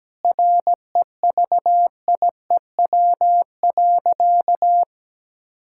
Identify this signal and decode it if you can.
Morse code, heard as a keyed tone, about 17 wpm, 700 Hz. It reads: REVIEW.
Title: Sentences from Top 1000 Words - Review Part 6 - 17wpm